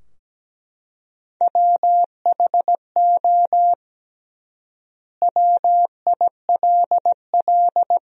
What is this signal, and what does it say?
Morse code, heard as a keyed tone, about 17 wpm, 700 Hz. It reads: WHO WILL
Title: 3x Word-Spacing - Sentences from Top 200 Words - 17wpm